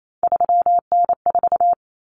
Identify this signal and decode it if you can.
Morse code, heard as a keyed tone, about 28 wpm, 700 Hz. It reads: 3N4
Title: Common Cut Numbers - 3-Digit 28wpm